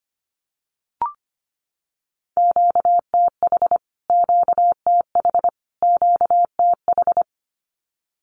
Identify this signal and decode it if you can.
Morse code, heard as a keyed tone, about 25 wpm, 700 Hz. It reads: QTH QTH QTH
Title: Q-Codes Repeated 3x 25wpm